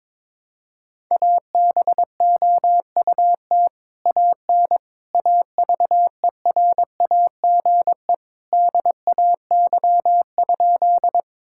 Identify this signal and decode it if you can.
Morse code, heard as a keyed tone, about 21 wpm, 700 Hz. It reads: ABOUT AN AVERAGE DAY?
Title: Sentences from Top 1000 Words - Review Part 6 - 22wpm